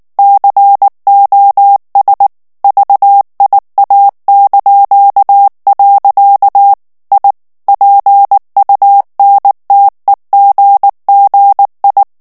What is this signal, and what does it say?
Morse code, heard as a keyed tone, about 19 wpm, 800 Hz. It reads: COS VIA). I PUNTEGGI
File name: morse mp3 / ITALIEN texte clair 2 vitesse 18 mots minute